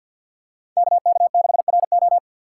Morse code, heard as a keyed tone, about 50 wpm, 690 Hz. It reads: KK6RO
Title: USA Call Signs - 5 Characters - 4x Character Spacing - 50wpm